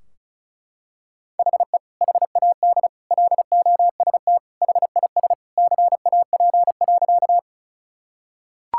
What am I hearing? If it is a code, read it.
Morse code, heard as a keyed tone, about 36 wpm, 700 Hz. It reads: HE HAD LOST HIS CAP.
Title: Sentences from Top 1600 Words 35wpm